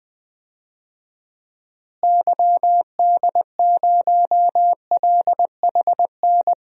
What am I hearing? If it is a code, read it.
Morse code, heard as a keyed tone, about 20 wpm, 700 Hz. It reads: YD0LHN
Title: DXCC-100 Call Signs - 6 Characters - Repeated 2x - 20wpm